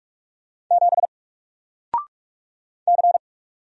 Morse code, heard as a keyed tone, about 44 wpm, 700 Hz. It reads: 7 C
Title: Single Character (Letter:Number:Punctuation:Prosign) 45wpm